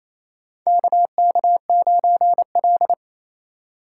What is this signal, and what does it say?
Morse code, heard as a keyed tone, about 28 wpm, 700 Hz. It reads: KK9L
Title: USA Call Signs - 4 Characters - 3x Character Spacing - 28wpm